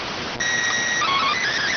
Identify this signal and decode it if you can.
SSTV (slow-scan television), PD 290.